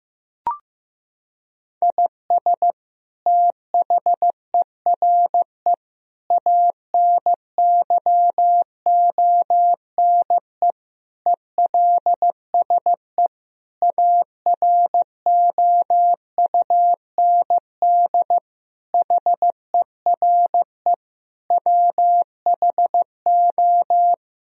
Morse code, heard as a keyed tone, about 16 wpm, 700 Hz. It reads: IS THERE ANYONE ELSE AROUND HERE WHO
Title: Sentences from Top 1000 Words - Review Part 3 - 15wpm